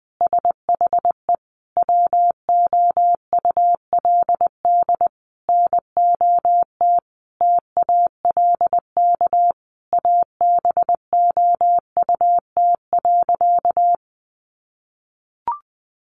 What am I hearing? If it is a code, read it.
Morse code, heard as a keyed tone, about 20 wpm, 700 Hz. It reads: SHE WOULD NOT TALK ABOUT.
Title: Sentences from Top 800 Words 20wpm